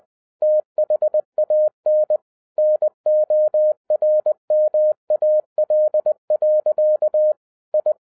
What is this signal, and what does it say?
Morse code, heard as a keyed tone, about 20 wpm, 610 Hz. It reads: THAN NORMAL. I
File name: 2011-10-mp3 / n03-TARO JR0QWW 20wpm